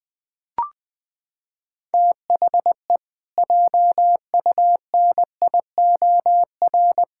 Morse code, heard as a keyed tone, about 20 wpm, 700 Hz. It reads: THE JUNIOR
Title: Pangram Challenge v2 - 20wpm